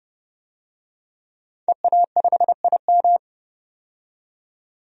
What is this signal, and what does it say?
Morse code, heard as a keyed tone, about 30 wpm, 700 Hz. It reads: EA5IM
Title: DXCC-100 Call Signs - 5 Characters - 30wpm